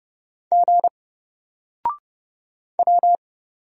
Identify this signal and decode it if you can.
Morse code, heard as a keyed tone, about 31 wpm, 700 Hz. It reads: G W
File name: Single-Letter-Number-30wpm